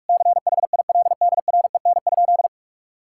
Morse code, heard as a keyed tone, about 46 wpm, 700 Hz. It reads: CHILDREN?
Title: Sentences from Top 1000 Words - Review Part 6 - 45wpm